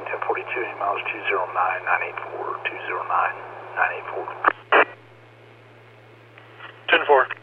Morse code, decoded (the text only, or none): none